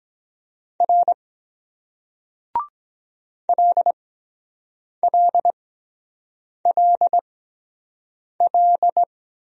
Morse code, heard as a keyed tone, about 23 wpm, spaced as for 7 wpm, 700 Hz. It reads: R LLLL